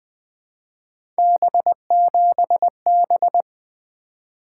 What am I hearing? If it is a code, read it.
Morse code, heard as a keyed tone, about 21 wpm, 700 Hz. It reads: B7B
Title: Three Letter-Number Combinations - 20wpm